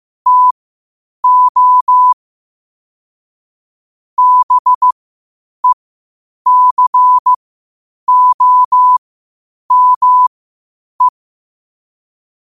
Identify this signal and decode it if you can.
Morse code, heard as a keyed tone, about 15 wpm, spaced as for 8 wpm, 1000 Hz. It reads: TO BECOME
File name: qotdfiles / 8wpm